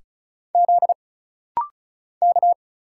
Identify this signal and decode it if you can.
Morse code, heard as a keyed tone, about 36 wpm, 700 Hz. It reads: Z K
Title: Single Letters - Rapid-Fire 35wpm